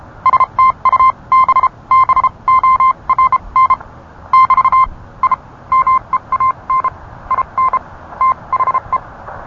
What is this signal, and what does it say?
Morse code, heard as a keyed tone, about 32 wpm, spaced as for 26 wpm, 1040 Hz. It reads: STUBBORN <BT> I MEAN IN THE